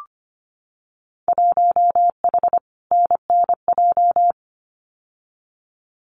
Morse code, heard as a keyed tone, about 25 wpm, 700 Hz. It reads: 1H NNJ
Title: Winter Field Day Exchanges 25wpm